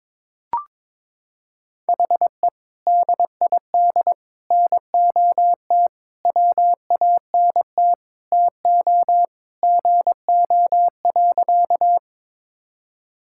Morse code, heard as a keyed tone, about 22 wpm, 700 Hz. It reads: HE DID NOT WANT TO GO.
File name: practice / 22wpm / Sentences with 3-Word Phrases - 22wpm